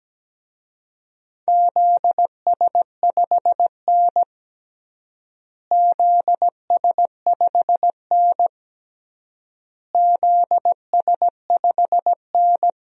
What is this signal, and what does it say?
Morse code, heard as a keyed tone, about 17 wpm, 700 Hz. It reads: ZS5N ZS5N ZS5N
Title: DXCC-100 Call Signs - 4 Characters - Repeated 3x - 17wpm